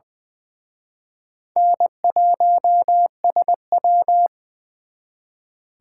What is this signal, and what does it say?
Morse code, heard as a keyed tone, about 23 wpm, 700 Hz. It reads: N1SW
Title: Speed Racing - USA 1x2 Call Signs 20wpm